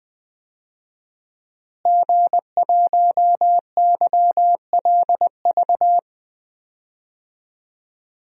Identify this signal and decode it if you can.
Morse code, heard as a keyed tone, about 20 wpm, 700 Hz. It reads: G1YLV
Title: DXCC-100 Call Signs - 5 Characters - Repeated 3x - 20wpm